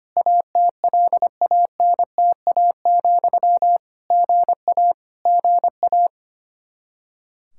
Morse code, heard as a keyed tone, about 25 wpm, 700 Hz. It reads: ATLANTA, GA GA
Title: K1USN SST QSO Practice 25wpm